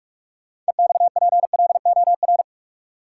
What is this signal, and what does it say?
Morse code, heard as a keyed tone, about 46 wpm, 700 Hz. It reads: EXPLOR
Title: Sentences from Top 1100 Words - New Word Drill Alt - 45wpm